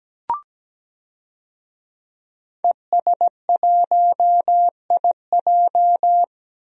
Morse code, heard as a keyed tone, about 17 wpm, 700 Hz. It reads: ES1IJ
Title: DXCC-100 Call Signs - 5 Characters - 17wpm